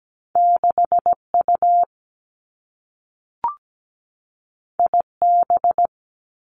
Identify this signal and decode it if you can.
Morse code, heard as a keyed tone, about 17 wpm, 700 Hz. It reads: 6U IB
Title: Two Challenging Letter-Number Combinations 17wpm